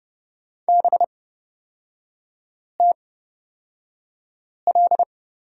Morse code, heard as a keyed tone, about 31 wpm, spaced as for 5 wpm, 700 Hz. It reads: BTL
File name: Lesson 36e - BT - Three Characters 30wpm